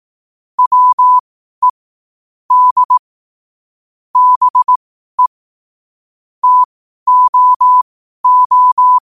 Morse code, heard as a keyed tone, about 18 wpm, spaced as for 12 wpm, 1000 Hz. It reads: WE D BE TOO
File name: qotdfiles / 12wpm